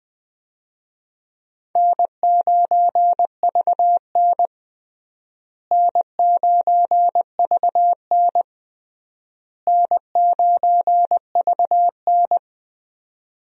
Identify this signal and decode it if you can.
Morse code, heard as a keyed tone, about 20 wpm, 700 Hz. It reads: N9VN N9VN N9VN
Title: USA Call Signs - 4 Characters - Repeated 3x - 20wpm